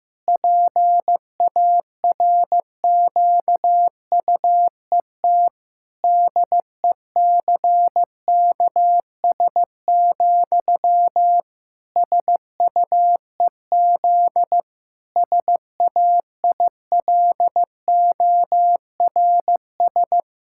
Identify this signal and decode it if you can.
Morse code, heard as a keyed tone, about 15 wpm, 700 Hz. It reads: PARQUET DECKS, SUEZ SAILORS
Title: Pangram Challenge v2 - 15wpm